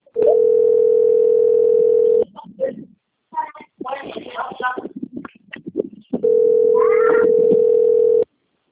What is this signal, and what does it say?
Ringback tone (North America), 440 + 480 Hz.
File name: rg-1600-unknown- - - .46256